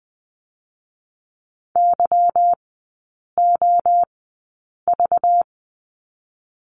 Morse code, heard as a keyed tone, about 19 wpm, spaced as for 8 wpm, 700 Hz. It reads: YOV
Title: 218 Lesson 35e - Number 0 - Three Characters 20wpm